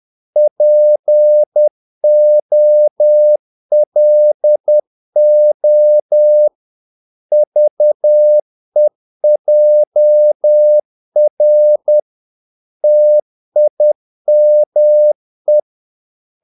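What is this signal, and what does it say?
Morse code, heard as a keyed tone, about 10 wpm, 600 Hz. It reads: POLO VEJR TIME
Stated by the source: Morse practice of short Danish words at 10 wpm, from CW med Gnister.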